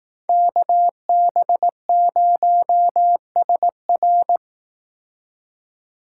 Morse code, heard as a keyed tone, about 18 wpm, 700 Hz. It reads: KB0SR
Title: Speed Racing - USA 2x2 Call Signs 15wpm